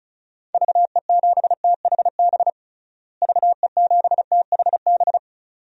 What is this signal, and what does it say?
Morse code, heard as a keyed tone, about 36 wpm, 700 Hz. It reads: VE7THB VE7THB
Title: DXCC-100 Call Signs - 6 Characters - Repeated 2x - 35wpm